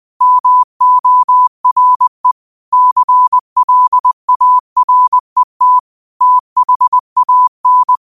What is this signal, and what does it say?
Morse code, heard as a keyed tone, about 20 wpm, 1000 Hz. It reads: MORE CLARET THAN
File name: qotdfiles / 20wpm